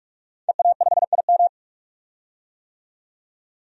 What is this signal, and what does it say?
Morse code, heard as a keyed tone, about 46 wpm, 700 Hz. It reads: EA5IM
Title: DXCC-100 Call Signs - 5 Characters - 45wpm